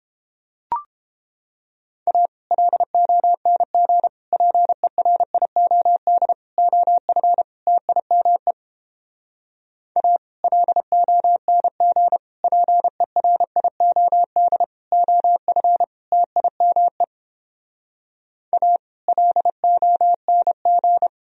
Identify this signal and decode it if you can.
Morse code, heard as a keyed tone, about 29 wpm, 700 Hz. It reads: A LONG PERIOD OF TIME A LONG PERIOD OF TIME A LONG